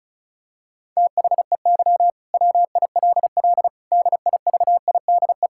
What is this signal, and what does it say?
Morse code, heard as a keyed tone, about 36 wpm, 690 Hz. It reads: THEY WILL DIVIDE
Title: Sentences from Top 1600 Words 35wpm